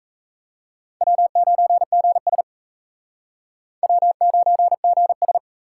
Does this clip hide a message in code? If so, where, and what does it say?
Yes: Morse code from 1.0 to 5.4 s, heard as a keyed tone, about 36 wpm, 700 Hz. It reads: W9GS W9GS